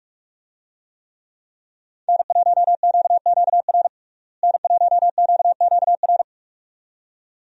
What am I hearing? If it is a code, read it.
Morse code, heard as a keyed tone, about 46 wpm, 700 Hz. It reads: N1QQR N1QQR
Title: USA Call Signs - 5 Characters - Repeated 2x - 45wpm